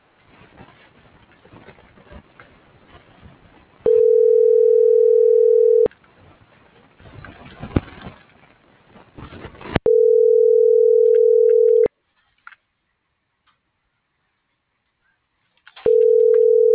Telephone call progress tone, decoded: ringback tone

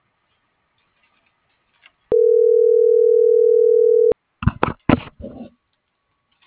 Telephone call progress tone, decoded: ringback tone

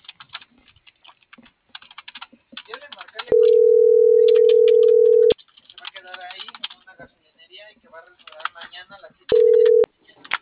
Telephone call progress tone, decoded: ringback tone

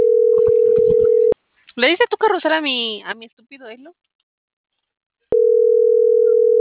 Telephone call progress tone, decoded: ringback tone

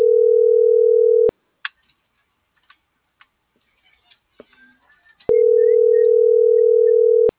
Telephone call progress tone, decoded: ringback tone

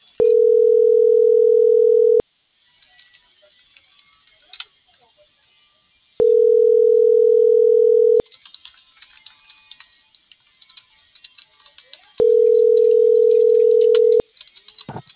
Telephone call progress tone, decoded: ringback tone

